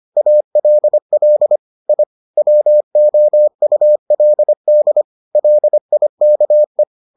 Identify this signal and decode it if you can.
Morse code, heard as a keyed tone, about 25 wpm, 600 Hz. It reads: ALL I WOULD LIKE